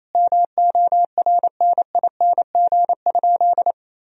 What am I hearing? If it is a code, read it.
Morse code, heard as a keyed tone, about 28 wpm, 700 Hz. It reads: MORNING?
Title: Sentences from Top 1000 Words - Review Part 2 - 28wpm